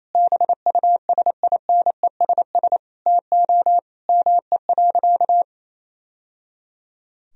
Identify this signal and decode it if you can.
Morse code, heard as a keyed tone, about 27 wpm, 700 Hz. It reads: BUSINESS TO ME.
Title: Sentences from Top 1000 Words - Review Part 3 - 28wpm